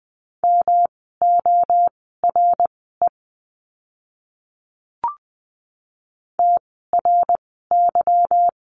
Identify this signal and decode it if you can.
Morse code, heard as a keyed tone, about 20 wpm, spaced as for 14 wpm, 700 Hz. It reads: MORE TRY
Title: 160 Lesson 26f - Letter g - Words 20wpm